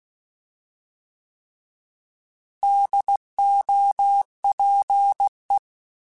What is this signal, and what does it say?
Morse code, heard as a keyed tone, about 16 wpm, 800 Hz. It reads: DOPE
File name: Alphabetical / Alphabetical mp3s / Ds @15